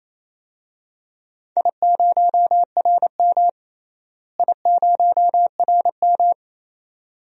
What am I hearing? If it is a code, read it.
Morse code, heard as a keyed tone, about 28 wpm, 700 Hz. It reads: I0RM I0RM
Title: DXCC-100 Call Signs - 4 Characters - Repeated 2x - 28wpm